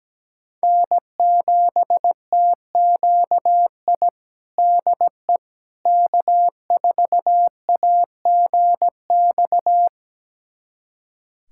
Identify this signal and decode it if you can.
Morse code, heard as a keyed tone, about 17 wpm, 700 Hz. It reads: N7TQI DE K4AGX
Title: USA Callsign DE Callsign 17wpm